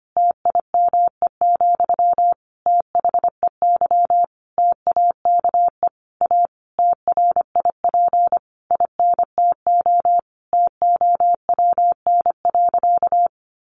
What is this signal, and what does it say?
Morse code, heard as a keyed tone, about 25 wpm, 700 Hz. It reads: TIME, THEY TAKE A TRIP INTO TOWN.
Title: Sentences from Top 1300 Words 25wpm